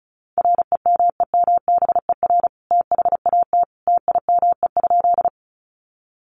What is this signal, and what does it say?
Morse code, heard as a keyed tone, about 34 wpm, 700 Hz. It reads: REMEMBER THAT TIME?
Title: Sentences from Top 400 Words 35wpm